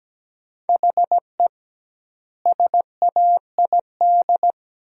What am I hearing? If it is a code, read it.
Morse code, heard as a keyed tone, about 17 wpm, 700 Hz. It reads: HE SAID
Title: 2x Word-Spacing - Sentences from Top 100 Words - 17wpm